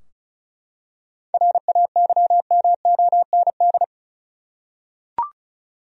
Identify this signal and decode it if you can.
Morse code, heard as a keyed tone, about 36 wpm, 700 Hz. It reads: RAYMOND
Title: USA Common Names - 35wpm